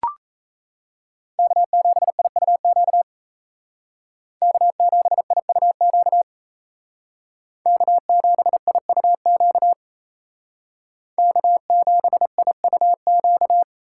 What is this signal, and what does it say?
Morse code, heard as a keyed tone, about 34 wpm, 700 Hz. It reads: K7IUQ K7IUQ K7IUQ K7IUQ
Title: Speed Racing - USA 1x3 Call Signs 28wpm